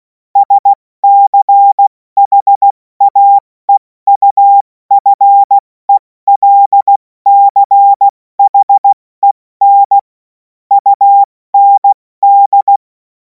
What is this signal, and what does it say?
Morse code, heard as a keyed tone, about 16 wpm, spaced as for 14 wpm, 800 Hz. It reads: SCHAEUFELCHEN UND
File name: Morsekurs / Witze / Witz 03. März 2026 70 F80